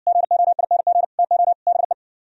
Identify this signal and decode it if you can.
Morse code, heard as a keyed tone, about 58 wpm, 700 Hz. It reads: GOING TO BE